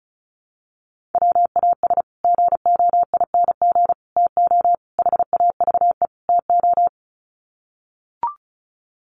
Morse code, heard as a keyed tone, about 34 wpm, 700 Hz. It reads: WAS GOING TO HAVE TO
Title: Sets of 5 Words Encore 35wpm